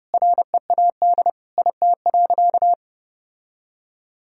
Morse code, heard as a keyed tone, about 32 wpm, 700 Hz. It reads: READ IT.
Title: Sentences from Top 1000 Words - Review Part 4 - 30wpm